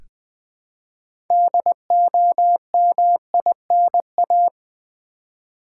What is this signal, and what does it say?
Morse code, heard as a keyed tone, about 20 wpm, 700 Hz. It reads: DOMINA
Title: Build A Word - 3 Letters At A Time - 8-Letter Words - 20wpm